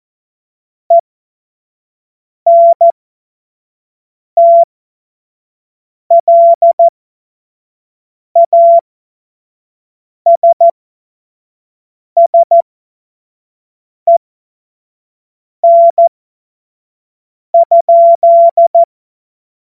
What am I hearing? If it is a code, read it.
Morse code, heard as a keyed tone, about 14 wpm, spaced as for 5 wpm, 680 Hz. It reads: ENTLASSEN?
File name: Morsekurs / Witze / Witz 16. Mai 2023 25 F70